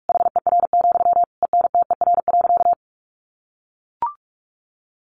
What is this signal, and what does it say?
Morse code, heard as a keyed tone, about 44 wpm, 700 Hz. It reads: HER, ENTER.
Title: Sentences from Top 1300 Words 45wpm